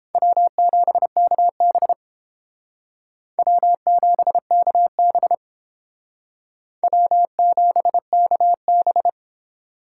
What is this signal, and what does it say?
Morse code, heard as a keyed tone, about 31 wpm, 700 Hz. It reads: W7KB W7KB W7KB